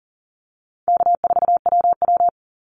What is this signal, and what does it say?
Morse code, heard as a keyed tone, about 40 wpm, 700 Hz. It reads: K4WW
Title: Easy Call Signs Repeated 2x - 40wpm